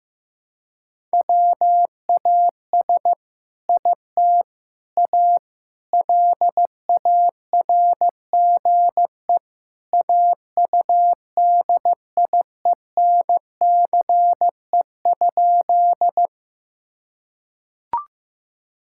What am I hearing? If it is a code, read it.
Morse code, heard as a keyed tone, about 16 wpm, 700 Hz. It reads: WAS IT A LARGE AUDIENCE?
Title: Sentences from Top 1000 Words - Review Part 5 - 15wpm